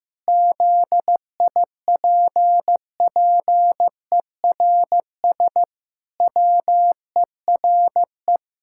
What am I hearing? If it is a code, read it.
Morse code, heard as a keyed tone, about 15 wpm, 700 Hz. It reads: ZIPPERS WERE
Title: Pangram Challenge v2 - 15wpm